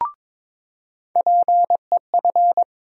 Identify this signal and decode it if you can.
Morse code, heard as a keyed tone, about 22 wpm, 700 Hz. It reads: PEF
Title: USA Call Sign Postfixes - 3 Letters 22wpm